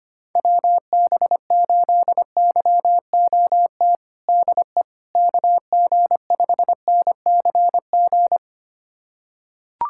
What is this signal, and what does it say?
Morse code, heard as a keyed tone, about 25 wpm, 700 Hz. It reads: WB8YOT DE KG5NCG
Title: USA Callsign DE Callsign 25wpm